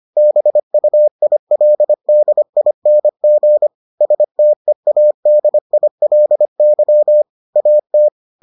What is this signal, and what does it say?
Morse code, heard as a keyed tone, about 25 wpm, 600 Hz. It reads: BUILDING STEADILY AT